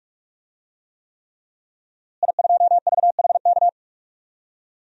Morse code, heard as a keyed tone, about 46 wpm, 700 Hz. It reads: I2VHK
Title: DXCC-100 Call Signs - 5 Characters - Repeated 3x - 45wpm